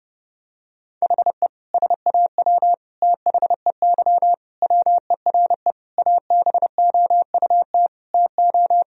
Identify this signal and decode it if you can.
Morse code, heard as a keyed tone, about 30 wpm, 700 Hz. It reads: HE SAW THEY WERE ABOUT TO
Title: Sentences from Top 1000 Words - Review Part 4 - 30wpm